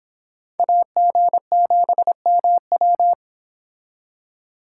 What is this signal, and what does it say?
Morse code, heard as a keyed tone, about 26 wpm, 700 Hz. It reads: AG7MW